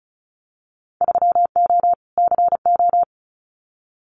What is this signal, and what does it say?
Morse code, heard as a keyed tone, about 36 wpm, 700 Hz. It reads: 3O CO
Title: Winter Field Day Exchanges Encore 35wpm